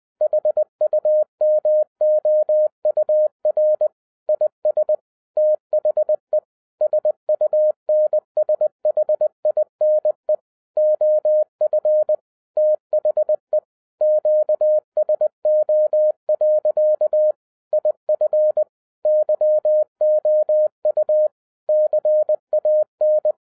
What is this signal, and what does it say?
Morse code, heard as a keyed tone, about 20 wpm, 610 Hz. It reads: HUMOUR IS THE SUNSHINE OF THE QSO. IF YOU CAN
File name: 2011-10-mp3 / n13-TIPS FROM MY 34 YEARS OF EXPERIENCE 20wpm